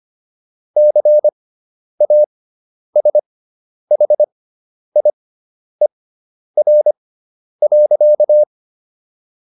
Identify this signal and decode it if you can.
Morse code, heard as a keyed tone, about 25 wpm, spaced as for 10 wpm, 600 Hz. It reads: CASHIER.